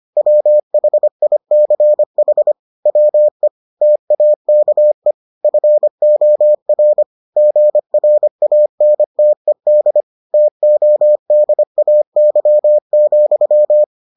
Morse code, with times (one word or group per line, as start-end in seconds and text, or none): 0.2-2.5 WHICH
2.9-3.5 WE
3.8-5.1 TAKE
5.4-7.0 FOR
7.4-10.0 GRANTED
10.3-13.8 TODAY,